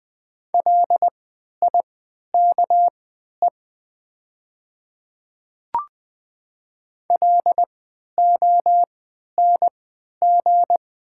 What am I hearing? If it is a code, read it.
Morse code, heard as a keyed tone, about 20 wpm, spaced as for 11 wpm, 700 Hz. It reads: LIKE LONG
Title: 3x Character Spacing - Top 100 Words - 20wpm